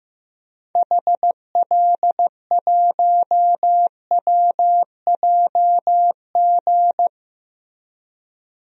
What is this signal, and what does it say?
Morse code, heard as a keyed tone, about 15 wpm, 700 Hz. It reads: HL1WJG
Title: DXCC-100 Call Signs - 6 Characters - Repeated 2x - 15wpm